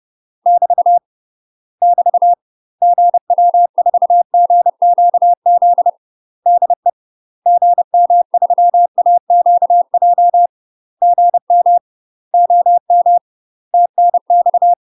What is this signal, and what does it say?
Morse code, heard as a keyed tone, about 30 wpm, 700 Hz. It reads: <BT> <BT> GW4GQZ DE GM3AQJ GM OM TNX
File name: UK MASTER QSOs TRAINING / 30WPM / UK MASTER QSOs 0000